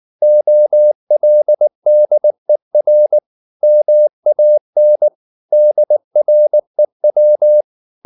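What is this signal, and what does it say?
Morse code, heard as a keyed tone, about 19 wpm, 600 Hz. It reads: OLDER MAN DREW
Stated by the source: Morse code at 19 wpm from CW med Gnister, reading Ken Follett's novel Never in English.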